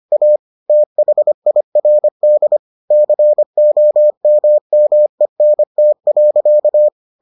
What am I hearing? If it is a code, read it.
Morse code, heard as a keyed tone, about 25 wpm, 600 Hz. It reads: A THIRD COMMENT.